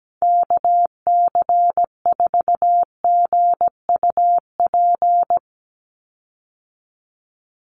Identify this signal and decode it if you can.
Morse code, heard as a keyed tone, about 17 wpm, 700 Hz. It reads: KC4GUP
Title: USA Call Signs 17wpm